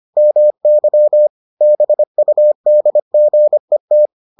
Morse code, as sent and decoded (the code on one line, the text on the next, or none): -- -.-- / -... ..- -.. --. -
MY BUDGET